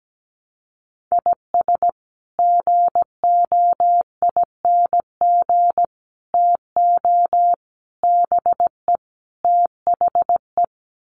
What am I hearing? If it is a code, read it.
Morse code, heard as a keyed tone, about 17 wpm, 700 Hz. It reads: IS GOING TO BE THE
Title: Sets of 5 Words 17wpm